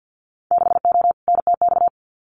Morse code, heard as a keyed tone, about 50 wpm, 700 Hz. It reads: 6O NTX
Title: Winter Field Day Exchanges Encore 50wpm